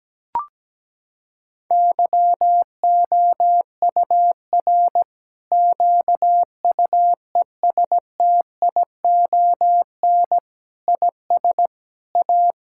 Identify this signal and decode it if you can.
Morse code, heard as a keyed tone, about 17 wpm, 700 Hz. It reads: YOUR QUESTION IS A
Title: Sentences from Top 1000 Words - Review Part 6 - 17wpm